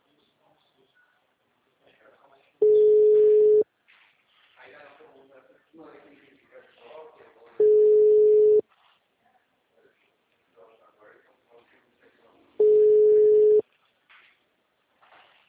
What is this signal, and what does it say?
Ringback tone (Europe, 425 Hz), with speech.
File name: out- -2029- - - .16927